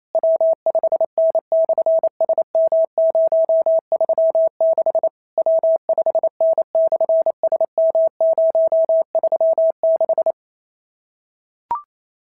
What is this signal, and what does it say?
Morse code, heard as a keyed tone, about 28 wpm, 650 Hz. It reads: W5N/SM036 W5N/SM036
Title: SOTA QSO Practice 28wpm